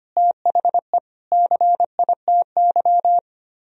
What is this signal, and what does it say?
Morse code, heard as a keyed tone, about 32 wpm, 700 Hz. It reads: THE CITY